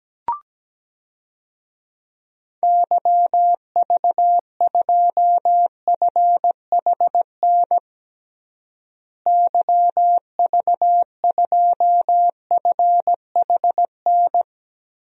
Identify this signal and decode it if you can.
Morse code, heard as a keyed tone, about 17 wpm, 700 Hz. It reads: YV2FHN YV2FHN
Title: DXCC-100 Call Signs - 6 Characters - Repeated 3x - 17wpm